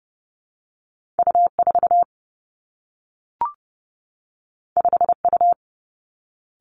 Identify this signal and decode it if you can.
Morse code, heard as a keyed tone, about 30 wpm, 700 Hz. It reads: U 4 5 U
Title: Two Challenging Letter-Number Combinations 30wpm